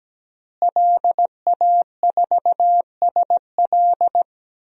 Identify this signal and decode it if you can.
Morse code, heard as a keyed tone, about 17 wpm, 700 Hz. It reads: LA4SL
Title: DXCC-100 Call Signs - 5 Characters - Repeated 3x - 17wpm